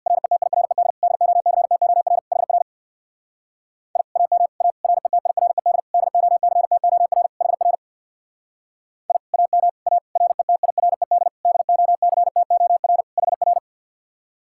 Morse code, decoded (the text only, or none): RETIRED DOCTOR HR I AM A RETIRED DOCTOR HR I AM A RETIRED DOCTOR HR